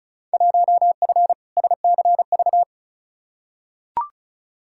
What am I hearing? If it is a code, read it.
Morse code, heard as a keyed tone, about 36 wpm, 700 Hz. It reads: 1F SCV
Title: Field Day Exchanges Encore - 35wpm